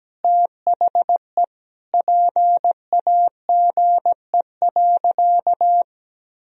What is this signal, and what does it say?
Morse code, heard as a keyed tone, about 17 wpm, 700 Hz. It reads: THE PAGE.
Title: Sentences from Top 1000 Words - Review Part 1 - 17wpm